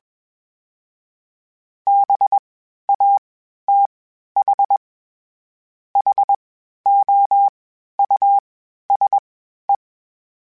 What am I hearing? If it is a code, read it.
Morse code, heard as a keyed tone, about 21 wpm, spaced as for 11 wpm, 800 Hz. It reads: BATHHOUSE
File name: General / General-mp3s / Houses @20